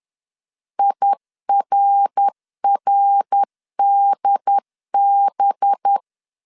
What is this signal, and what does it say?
Morse code, heard as a keyed tone, about 10 wpm, 800 Hz. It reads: IRRDB
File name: CW / DET / 1 6 1500C S B 10wpm 4137